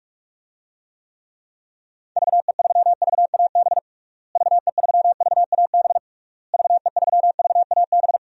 Morse code, heard as a keyed tone, about 46 wpm, 700 Hz. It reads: VE3VAB VE3VAB VE3VAB
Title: DXCC-100 Call Signs - 6 Characters - Repeated 3x - 45wpm